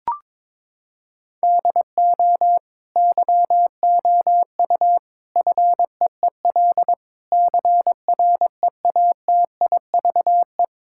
Morse code, heard as a keyed tone, about 22 wpm, 700 Hz. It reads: DO YOU FEEL CREATIVE
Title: Sentences from Top 1000 Words - Review Part 5 - 22wpm